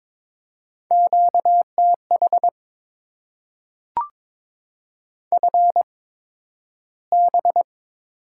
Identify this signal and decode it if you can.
Morse code, heard as a keyed tone, about 22 wpm, spaced as for 12 wpm, 700 Hz. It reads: QTH FB
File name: Top 20 QSO Elements - 8x Character Spacing - 22wpm